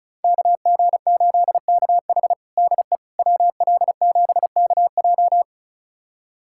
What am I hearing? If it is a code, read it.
Morse code, heard as a keyed tone, about 36 wpm, 690 Hz. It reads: KG8KH DE WL7KJ
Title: USA Callsign DE Callsign - 5 Characters - 35wpm